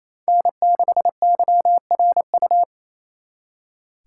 Morse code, heard as a keyed tone, about 33 wpm, 700 Hz. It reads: N6YRU
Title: Speed Racing - USA 1x3 Call Signs 28wpm